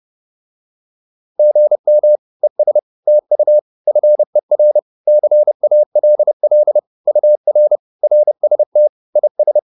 Morse code, heard as a keyed tone, about 30 wpm, 600 Hz. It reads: GM ES TU FER CALL UR RST IS
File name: QSO109 30